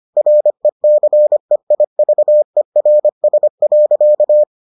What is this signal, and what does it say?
Morse code, heard as a keyed tone, about 25 wpm, 600 Hz. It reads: RECEIVERS.